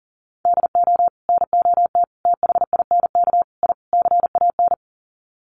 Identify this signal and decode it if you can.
Morse code, heard as a keyed tone, about 40 wpm, 700 Hz. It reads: DO NOT THINK I CAN
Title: Sets of 5 Words Encore2 40wpm